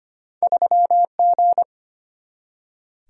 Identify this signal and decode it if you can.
Morse code, heard as a keyed tone, about 25 wpm, 700 Hz. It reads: 3G
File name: Two Letter-Number Combinations - 25wpm